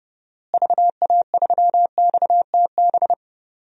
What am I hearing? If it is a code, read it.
Morse code, heard as a keyed tone, about 30 wpm, 700 Hz. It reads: VA3XTB
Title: DXCC-100 Call Signs - 6 Characters - Repeated 3x - 30wpm